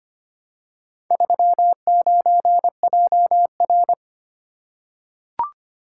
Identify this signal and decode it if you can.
Morse code, heard as a keyed tone, about 25 wpm, 700 Hz. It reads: 39JR
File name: practice / 25wpm / Four Letter-Number Combinations - 25wpm